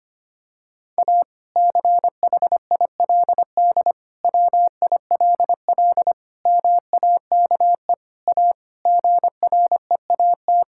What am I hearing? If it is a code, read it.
Morse code, heard as a keyed tone, about 25 wpm, 700 Hz. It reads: A CHILD WILL MAKE A GREAT
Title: Sentences from Top 1000 Words - Review Part 6 - 25wpm